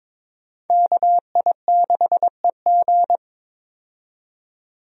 Morse code, heard as a keyed tone, about 22 wpm, 700 Hz. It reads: KI6EG
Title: USA Call Signs - 5 Characters - 22wpm